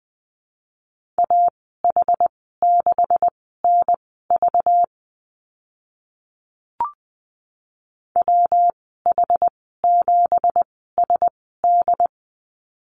Morse code, heard as a keyed tone, about 20 wpm, spaced as for 14 wpm, 700 Hz. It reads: AH6NV WH7SD
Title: 193 Lesson 31g - Letter v - Call Signs 20wpm